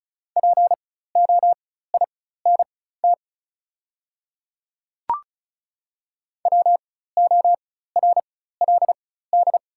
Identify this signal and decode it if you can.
Morse code, heard as a keyed tone, about 36 wpm, spaced as for 16 wpm, 690 Hz. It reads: POINT WORLD
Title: 4x Character Spacing - Top 300 Words - 35wpm